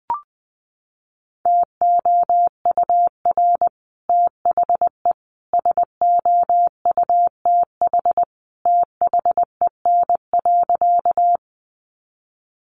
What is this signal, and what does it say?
Morse code, heard as a keyed tone, about 20 wpm, 700 Hz. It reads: TOUR THE SOUTH THEN.
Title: Sentences from Top 1400 Words 20wpm